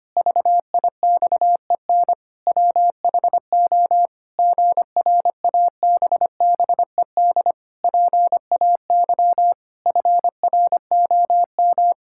Morse code, heard as a keyed tone, about 25 wpm, 700 Hz. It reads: VIXEN WHO GRABBED PAY FROM
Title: Pangram Challenge v2 - 25wpm